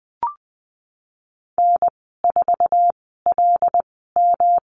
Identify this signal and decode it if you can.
Morse code, heard as a keyed tone, about 20 wpm, spaced as for 15 wpm, 700 Hz. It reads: N4LM
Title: 200 Lesson 32g - Letter k - Call Signs 20wpm